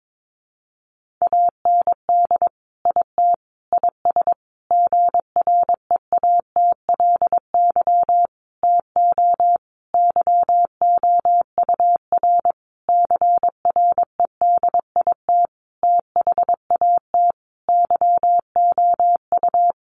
Morse code, heard as a keyed tone, about 22 wpm, 700 Hz. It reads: AND IT IS GREATLY TO YOUR CREDIT THAT YOU
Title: Sentences from Top 1400 Words 22wpm